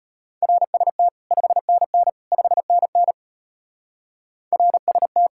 Morse code, heard as a keyed tone, about 32 wpm, 700 Hz. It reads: RST 5NN 5NN RST